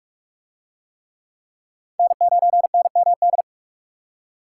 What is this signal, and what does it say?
Morse code, heard as a keyed tone, about 40 wpm, 700 Hz. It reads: N9NMD